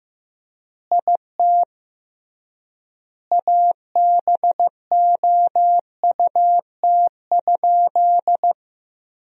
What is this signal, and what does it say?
Morse code, heard as a keyed tone, about 15 wpm, 700 Hz. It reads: IT ABOUT?
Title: 3x Word-Spacing - Sentences from Top 100 Words - 15wpm